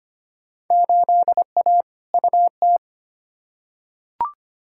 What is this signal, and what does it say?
Morse code, heard as a keyed tone, about 25 wpm, 700 Hz. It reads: 8A UT
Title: Field Day Exchanges Encore - 25wpm